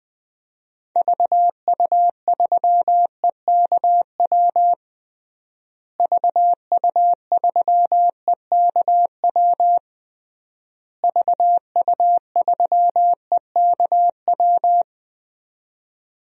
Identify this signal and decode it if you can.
Morse code, heard as a keyed tone, about 20 wpm, 700 Hz. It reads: VU3EKW VU3EKW VU3EKW
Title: DXCC-100 Call Signs - 6 Characters - Repeated 3x - 20wpm